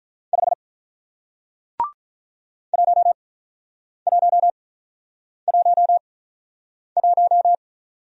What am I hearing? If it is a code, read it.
Morse code, heard as a keyed tone, about 46 wpm, spaced as for 9 wpm, 700 Hz. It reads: 5 1111